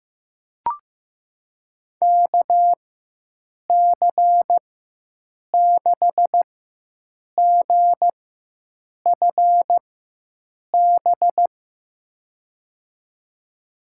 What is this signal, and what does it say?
Morse code, heard as a keyed tone, about 15 wpm, spaced as for 7 wpm, 700 Hz. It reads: KC6GFB
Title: USA Call Signs - 4x Character Spacing - 15wpm